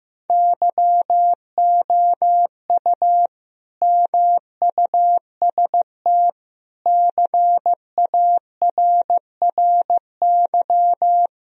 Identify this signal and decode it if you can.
Morse code, heard as a keyed tone, about 15 wpm, 700 Hz. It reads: YOU MUST CARRY
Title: Sentences from Top 1000 Words - Review Part 1 - 15wpm